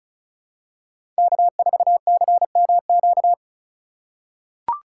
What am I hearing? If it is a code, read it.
Morse code, heard as a keyed tone, about 36 wpm, 700 Hz. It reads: K4CMQ
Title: USA Call Signs - 3x Character Spacing - 35wpm